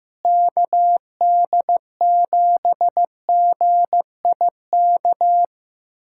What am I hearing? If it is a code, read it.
Morse code, heard as a keyed tone, about 15 wpm, 700 Hz. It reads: KD7GIK
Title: USA Callsign DE Callsign - 6 Characters - 15wpm